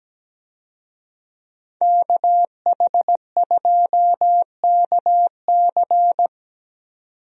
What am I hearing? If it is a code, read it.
Morse code, heard as a keyed tone, about 17 wpm, 700 Hz. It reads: KH2KC
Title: USA Call Signs - 5 Characters - 17wpm